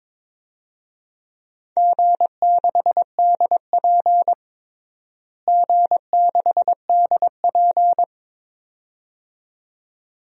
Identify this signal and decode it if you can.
Morse code, heard as a keyed tone, about 22 wpm, 700 Hz. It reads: G6DP G6DP
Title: DXCC-100 Call Signs - 4 Characters - Repeated 2x - 22wpm